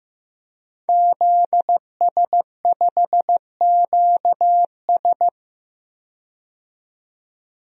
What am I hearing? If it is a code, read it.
Morse code, heard as a keyed tone, about 15 wpm, 700 Hz. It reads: ZS5QS
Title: DXCC-100 Call Signs - 5 Characters - 15wpm